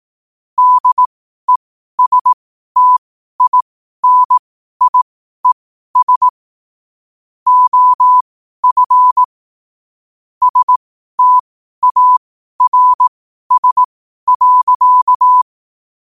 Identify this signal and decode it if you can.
Morse code, heard as a keyed tone, about 18 wpm, spaced as for 12 wpm, 1000 Hz. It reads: DESTINIES OF STARS.
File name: qotdfiles / 12wpm